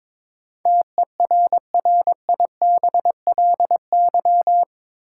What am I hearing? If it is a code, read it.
Morse code, heard as a keyed tone, about 22 wpm, 700 Hz. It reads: TERRIBLY
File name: practice / 22wpm / Eight Letter Words - 22wpm